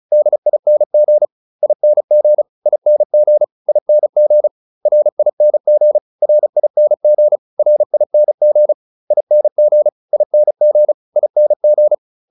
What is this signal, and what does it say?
Morse code, heard as a keyed tone, about 34 wpm, 600 Hz. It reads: DING ING ING ING RING RING RING ING ING ING